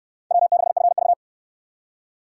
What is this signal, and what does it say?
Morse code, heard as a keyed tone, about 61 wpm, 700 Hz. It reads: W6FV